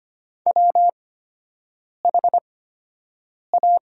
Morse code, heard as a keyed tone, about 25 wpm, spaced as for 7 wpm, 700 Hz. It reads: WHA
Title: 8x Character Spacing - Top 100 Words - 25wpm